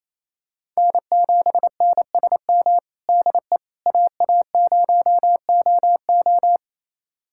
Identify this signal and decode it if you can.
Morse code, heard as a keyed tone, about 28 wpm, 700 Hz. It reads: N7NSM DE AA0OO
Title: USA Callsign DE Callsign - 5 Characters - 28wpm